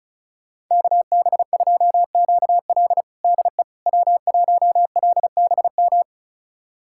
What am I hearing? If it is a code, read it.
Morse code, heard as a keyed tone, about 36 wpm, 700 Hz. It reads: KB2QL DE W1LBM
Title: USA Callsign DE Callsign - 5 Characters - 35wpm